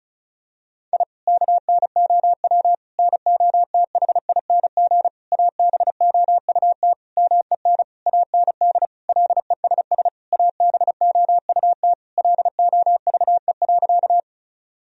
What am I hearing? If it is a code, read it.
Morse code, heard as a keyed tone, about 36 wpm, 700 Hz. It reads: I KNOW NOTHING ABOUT MEN AND LESS ABOUT LOVE.
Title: Sentences from Top 1000 Words - Review Part 2 - 35wpm